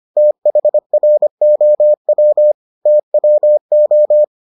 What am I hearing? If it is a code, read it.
Morse code, heard as a keyed tone, about 25 wpm, 600 Hz. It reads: THROW TWO